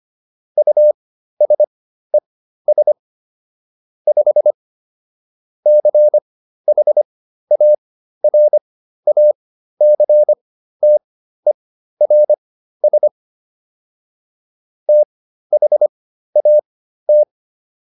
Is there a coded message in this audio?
Yes: Morse code from 0.6 to 17.2 s, heard as a keyed tone, about 25 wpm, spaced as for 13 wpm, 600 Hz. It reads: USES 5 CHARACTERS THAT